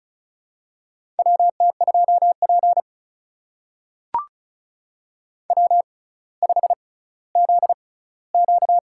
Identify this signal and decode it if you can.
Morse code, heard as a keyed tone, about 36 wpm, spaced as for 18 wpm, 700 Hz. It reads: WT2P W5ZQ
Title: USA Call Signs - 4 Characters - 6x Character Spacing - 35wpm